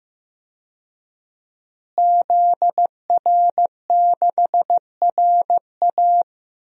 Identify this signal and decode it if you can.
Morse code, heard as a keyed tone, about 16 wpm, 700 Hz. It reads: ZR6RA
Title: DXCC-100 Call Signs - 5 Characters - Repeated 3x - 15wpm